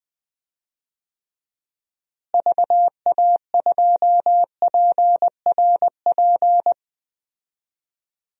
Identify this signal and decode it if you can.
Morse code, heard as a keyed tone, about 20 wpm, 700 Hz. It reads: VA2PRP
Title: DXCC-100 Call Signs - 6 Characters - Repeated 2x - 20wpm